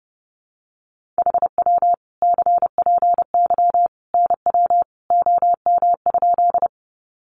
Morse code, heard as a keyed tone, about 28 wpm, 700 Hz. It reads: HW CPY NW OM?
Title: Common QSO Phrases 30wpm